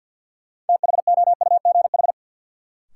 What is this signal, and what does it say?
Morse code, heard as a keyed tone, about 53 wpm, 700 Hz. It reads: THOUGH
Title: Sentences from Top 400 Words - New Word Drill - 50wpm